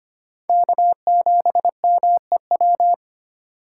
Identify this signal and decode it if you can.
Morse code, heard as a keyed tone, about 25 wpm, 700 Hz. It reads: K7MEW
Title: USA Call Signs - 3x Character Spacing - 25wpm